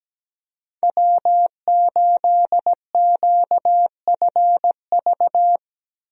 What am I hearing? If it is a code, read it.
Morse code, heard as a keyed tone, about 17 wpm, 700 Hz. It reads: W8QFV
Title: USA Call Signs - 5 Characters - Repeated 3x - 17wpm